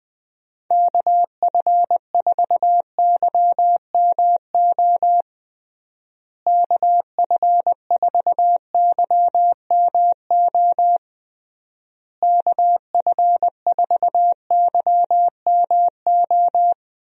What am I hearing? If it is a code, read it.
Morse code, heard as a keyed tone, about 21 wpm, 700 Hz. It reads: KF4YMO KF4YMO KF4YMO
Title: USA Call Signs - 6 Characters - Repeated 3x - 20wpm